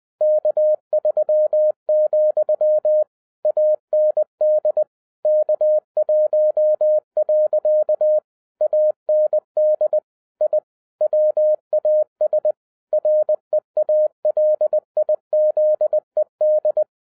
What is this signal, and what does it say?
Morse code, heard as a keyed tone, about 20 wpm, 610 Hz. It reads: K3, AND K1. AND I WAS REALIZED
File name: 2011-07-mp3 / n58 BACK FROM A BIG ACCIDENT 20wpm